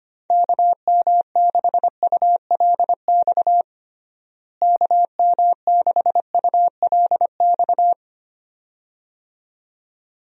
Morse code, heard as a keyed tone, about 25 wpm, 700 Hz. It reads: KM6ULX KM6ULX
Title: USA Call Signs - 6 Characters - Repeated 2x - 25wpm